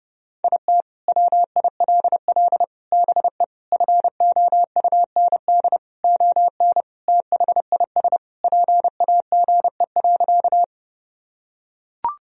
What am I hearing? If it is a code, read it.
Morse code, heard as a keyed tone, about 30 wpm, 700 Hz. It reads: IT WILL BE FOUND ON THIS PAGE.
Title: Sentences from Top 1000 Words - Review Part 1 - 30wpm